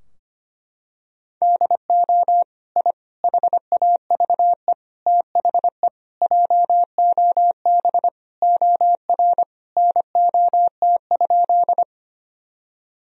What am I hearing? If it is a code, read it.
Morse code, heard as a keyed tone, about 25 wpm, 700 Hz. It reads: DO I HAVE THE JOB OR NOT?
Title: Sentences from Top 1000 Words - Review Part 3 - 25wpm